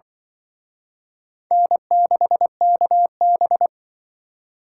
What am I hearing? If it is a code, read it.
Morse code, heard as a keyed tone, about 25 wpm, 700 Hz. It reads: N6KB